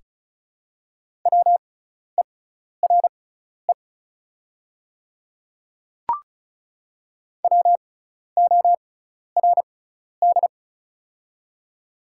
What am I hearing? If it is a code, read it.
Morse code, heard as a keyed tone, about 36 wpm, spaced as for 12 wpm, 700 Hz. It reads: WERE WORD
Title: 6x Character Spacing - Top 100 Words - 35wpm